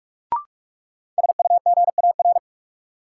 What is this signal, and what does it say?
Morse code, heard as a keyed tone, about 41 wpm, 700 Hz. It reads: SUGAR
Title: Sentences from Top 1200 Words - New Word Drill Alt - 45wpm